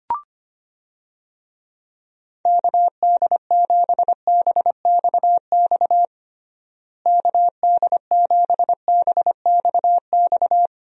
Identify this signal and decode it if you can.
Morse code, heard as a keyed tone, about 25 wpm, 700 Hz. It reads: KD7BXX KD7BXX
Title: USA Call Signs - 6 Characters - Repeated 3x - 25wpm